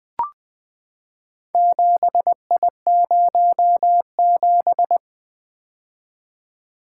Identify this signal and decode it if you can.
Morse code, heard as a keyed tone, about 20 wpm, 700 Hz. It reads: 7I07
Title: Four Letter-Number Combinations - 20wpm